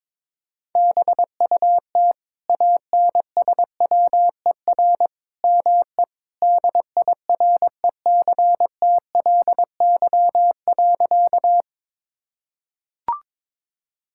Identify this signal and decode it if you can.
Morse code, heard as a keyed tone, about 21 wpm, 700 Hz. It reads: BUT ANSWER ME DIRECTLY.
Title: Sentences from Top 1000 Words - Review Part 5 - 22wpm